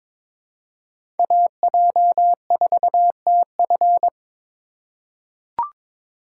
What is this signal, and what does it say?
Morse code, heard as a keyed tone, about 23 wpm, 700 Hz. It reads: AJ4TF
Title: K1USN SST QSO Practice - 22wpm